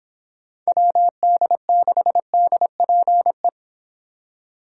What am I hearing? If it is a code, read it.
Morse code, heard as a keyed tone, about 26 wpm, 700 Hz. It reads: WD6DPE